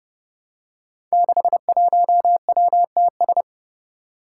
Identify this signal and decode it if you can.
Morse code, heard as a keyed tone, about 30 wpm, 700 Hz. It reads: 61WTS
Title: Five Challenging Letter-Number Combinations 30wpm